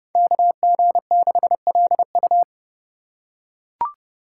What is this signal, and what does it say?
Morse code, heard as a keyed tone, about 30 wpm, 690 Hz. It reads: KG6LU
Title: USA Call Signs - 5 Characters - 2x Character Spacing - 30wpm